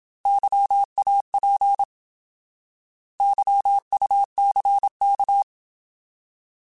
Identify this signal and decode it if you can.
Morse code, heard as a keyed tone, about 27 wpm, 800 Hz. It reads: YAP YUCK
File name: Alphabetical / Alphabetical mp3s / Ys @25